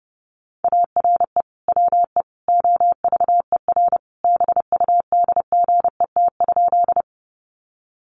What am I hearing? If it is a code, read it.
Morse code, heard as a keyed tone, about 30 wpm, 700 Hz. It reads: ARE WE OVER BUDGET?
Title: Sentences from Top 1200 Words 30wpm